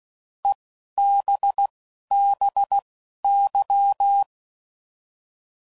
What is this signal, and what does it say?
Morse code, heard as a keyed tone, about 16 wpm, spaced as for 11 wpm, 800 Hz. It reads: EBBY